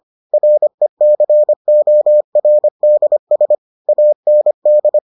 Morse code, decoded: RECORDS AND

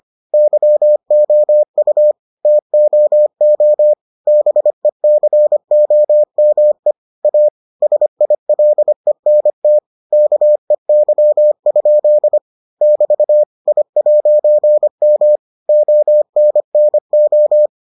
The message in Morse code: YOU TOO BECOME A SILENT KEY? <BT> I'M ONNO

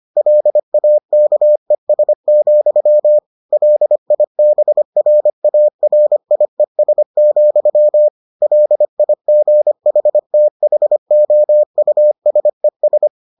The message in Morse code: LAKES, LIBRARIES, LIGHTHOUSES

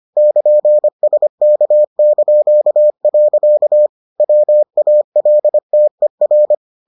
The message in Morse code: <KN>SK). WALTER

